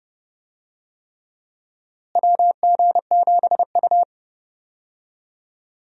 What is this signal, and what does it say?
Morse code, heard as a keyed tone, about 32 wpm, 700 Hz. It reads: WG7U
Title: USA Call Signs - 4 Characters - Repeated 3x - 30wpm